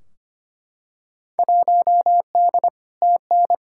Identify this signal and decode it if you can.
Morse code, heard as a keyed tone, about 25 wpm, 700 Hz. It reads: 1D TN
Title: Field Day Exchanges - 25wpm